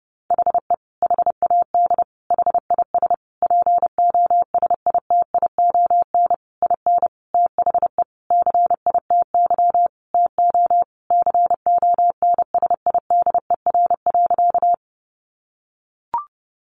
Morse code, heard as a keyed tone, about 30 wpm, 700 Hz. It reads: HE HAD HIS POSITION IN THE CITY TO CONSIDER.
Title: Sentences from Top 900 Words 30wpm